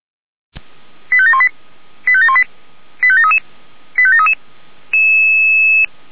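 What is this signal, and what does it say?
Five-tone selective call (ZVEI), heard as quick runs of five tones, tones of about 80 ms. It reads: 85718 85718 85733 85733
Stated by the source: ZVEI five-tone selective call received on an SDR.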